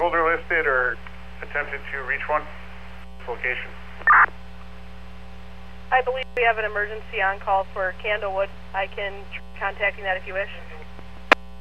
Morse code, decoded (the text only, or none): none